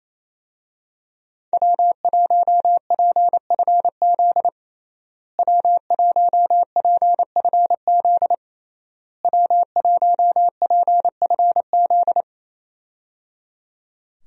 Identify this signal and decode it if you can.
Morse code, heard as a keyed tone, about 28 wpm, 700 Hz. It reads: W1PFZ W1PFZ W1PFZ
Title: USA Call Signs - 5 Characters - Repeated 3x - 28wpm